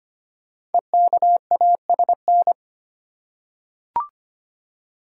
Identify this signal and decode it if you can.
Morse code, heard as a keyed tone, about 25 wpm, 700 Hz. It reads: EKASN
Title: Code Groups - 5 Letters - 25wpm